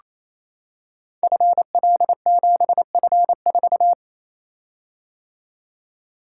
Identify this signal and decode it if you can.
Morse code, heard as a keyed tone, about 29 wpm, 700 Hz. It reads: FL7F4
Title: Code Groups - 5 Letter:Numbers - 28wpm